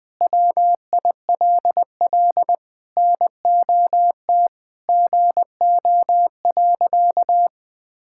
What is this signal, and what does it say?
Morse code, heard as a keyed tone, about 20 wpm, 700 Hz. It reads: WILL NOT GO.
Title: Sentences with 4-Word Phrases - Spoken Beginning - 20wpm